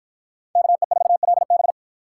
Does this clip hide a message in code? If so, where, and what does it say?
Yes: Morse code from 0.6 to 1.7 s, heard as a keyed tone, about 51 wpm, 700 Hz. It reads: KE4LB